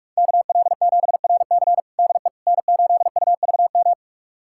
Morse code, heard as a keyed tone, about 43 wpm, 700 Hz. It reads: KL7RC DE N8UVM
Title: USA Callsign DE Callsign - 5 Characters - 45wpm